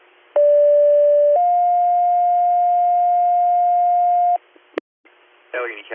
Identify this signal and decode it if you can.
Two-tone page, heard as a one-second tone, then a three-second tone, about 580 then 710 Hz, with speech. It is heard from 0.4 to 4.4 s.